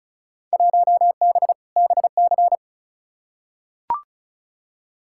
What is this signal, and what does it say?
Morse code, heard as a keyed tone, about 36 wpm, 700 Hz. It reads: 1B BC
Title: Field Day Exchanges - 35wpm